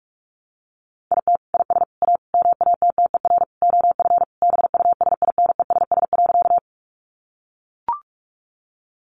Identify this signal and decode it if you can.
Morse code, heard as a keyed tone, about 44 wpm, 700 Hz. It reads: IT IS A MATTER OF BUSINESS.
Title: Sentences from Top 700 Words 45wpm